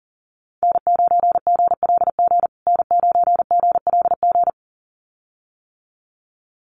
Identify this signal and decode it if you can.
Morse code, heard as a keyed tone, about 40 wpm, 700 Hz. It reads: N9GLG N9GLG
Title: USA Call Signs Repeated 2x 40wpm